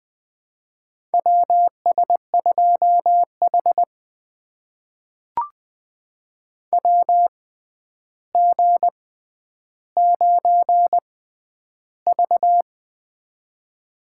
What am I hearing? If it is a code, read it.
Morse code, heard as a keyed tone, about 21 wpm, spaced as for 10 wpm, 700 Hz. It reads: WS2H WG9V
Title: USA Call Signs - 4 Characters - 6x Character Spacing - 20wpm